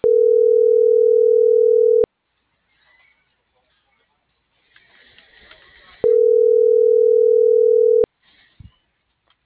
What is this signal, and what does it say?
Ringback tone (North America), 440 + 480 Hz.